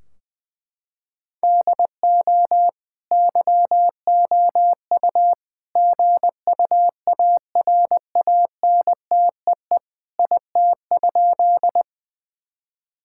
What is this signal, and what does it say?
Morse code, heard as a keyed tone, about 20 wpm, 700 Hz. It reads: DO YOU GUARANTEE IT?
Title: Sentences from Top 1600 Words 20wpm